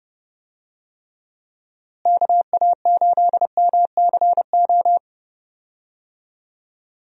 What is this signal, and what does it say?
Morse code, heard as a keyed tone, about 30 wpm, 700 Hz. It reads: KA8MCO
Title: USA Call Signs - 6 Characters - Repeated 3x - 30wpm